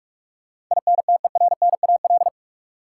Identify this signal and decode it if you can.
Morse code, heard as a keyed tone, about 41 wpm, 700 Hz. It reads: INTERNAL
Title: Sentences from Top 1400 Words - New Word Drill - 45wpm